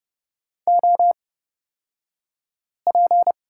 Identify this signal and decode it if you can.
Morse code, heard as a keyed tone, about 30 wpm, spaced as for 5 wpm, 700 Hz. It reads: OP
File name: Lesson 38d - Letter q - Two Characters 30wpm